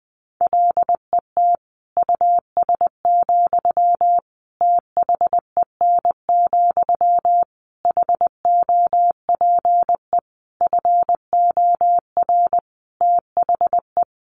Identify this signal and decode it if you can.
Morse code, heard as a keyed tone, about 20 wpm, 700 Hz. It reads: LET US, THEN, HOPE FOR THE
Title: Sentences from Top 1000 Words 20wpm